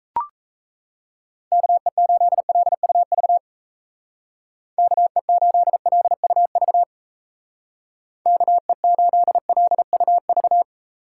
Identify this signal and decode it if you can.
Morse code, heard as a keyed tone, about 33 wpm, 700 Hz. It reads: KE8LUV KE8LUV KE8LUV
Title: Speed Racing - USA 2x3 Call Signs 28wpm